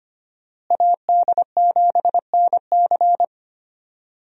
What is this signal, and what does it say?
Morse code, heard as a keyed tone, about 25 wpm, 700 Hz. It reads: AD7NC
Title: USA Call Signs - 5 Characters - Repeated 3x - 25wpm